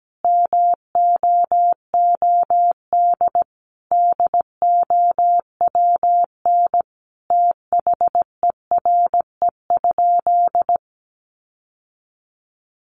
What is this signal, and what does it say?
Morse code, heard as a keyed tone, about 17 wpm, 700 Hz. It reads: MOOD DOWN THERE?
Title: Sentences from Top 1100 Words 17wpm